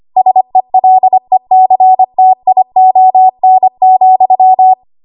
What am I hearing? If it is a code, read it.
Morse code, heard as a keyed tone, about 25 wpm, 750 Hz. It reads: SELECTION,
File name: Archive / 25 WPM / 25WPM